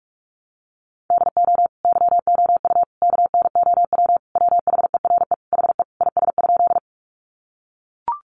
Morse code, heard as a keyed tone, about 44 wpm, 700 Hz. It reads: DO YOU KNOW WHERE HE IS?
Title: Sentences from Top 300 Words 45wpm